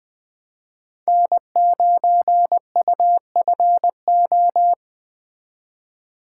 Morse code, heard as a keyed tone, about 21 wpm, 700 Hz. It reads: N9UFO